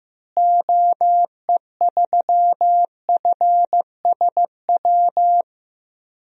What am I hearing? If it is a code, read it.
Morse code, heard as a keyed tone, about 15 wpm, 700 Hz. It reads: OE3FSW
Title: DXCC-100 Call Signs - 6 Characters - Repeated 3x - 15wpm